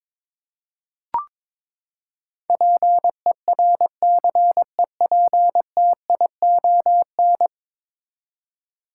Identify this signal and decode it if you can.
Morse code, heard as a keyed tone, about 22 wpm, 700 Hz. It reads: PERCEPTION
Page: morse-code-ninja-practice-22wpm